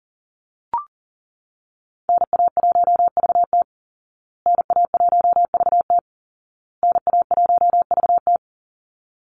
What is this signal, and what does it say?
Morse code, heard as a keyed tone, about 40 wpm, 700 Hz. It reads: NA1VT NA1VT NA1VT
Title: Call Signs Repeated 2x - 40wpm